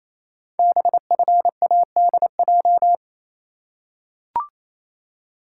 Morse code, heard as a keyed tone, about 29 wpm, 700 Hz. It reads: BFADJ
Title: Code Groups - 5 Letters - 28wpm